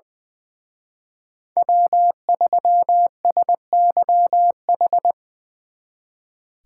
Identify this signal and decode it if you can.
Morse code, heard as a keyed tone, about 23 wpm, 700 Hz. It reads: W3SYH
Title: Speed Racing - USA 1x3 Call Signs 20wpm